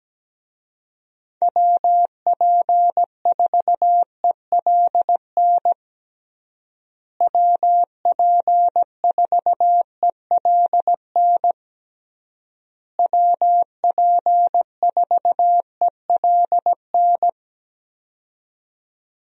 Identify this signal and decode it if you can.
Morse code, heard as a keyed tone, about 17 wpm, 700 Hz. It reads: WP4ELN WP4ELN WP4ELN
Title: USA Call Signs - 6 Characters - Repeated 3x - 17wpm